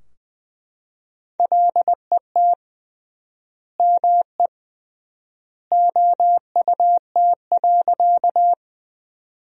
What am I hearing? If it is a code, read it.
Morse code, heard as a keyed tone, about 20 wpm, 700 Hz. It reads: LET ME OUT.
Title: 3x Word-Spacing - Sentences from Top 300 Words - 20wpm